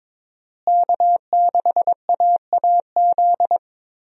Morse code, heard as a keyed tone, about 21 wpm, 700 Hz. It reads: K6AAZ
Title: USA Call Signs - 5 Characters - Repeated 3x - 22wpm